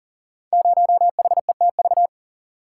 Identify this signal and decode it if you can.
Morse code, heard as a keyed tone, about 42 wpm, 690 Hz. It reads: 0HETV
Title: Code Groups - 5 Letter:Numbers - 40wpm